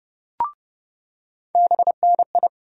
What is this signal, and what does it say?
Morse code, heard as a keyed tone, about 30 wpm, 700 Hz. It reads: BNI
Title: USA Call Sign Postfixes - 3 Letters 30wpm